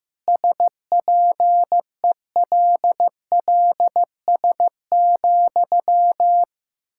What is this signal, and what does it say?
Morse code, heard as a keyed tone, about 16 wpm, 700 Hz. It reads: SPELLS,
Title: Pangram Challenge v2 - 15wpm